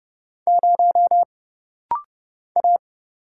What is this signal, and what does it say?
Morse code, heard as a keyed tone, about 30 wpm, 700 Hz. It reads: 0 A
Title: Single Letter-Number - Rapid-Fire 30wpm